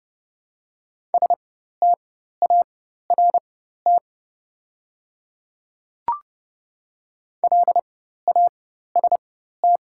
Morse code, heard as a keyed tone, about 30 wpm, spaced as for 14 wpm, 700 Hz. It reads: START LAST
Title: 4x Character Spacing - Top 300 Words - 30wpm